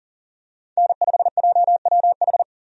Morse code, heard as a keyed tone, about 40 wpm, 700 Hz. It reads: N5JWH
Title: USA Call Signs - 5 Characters - 40wpm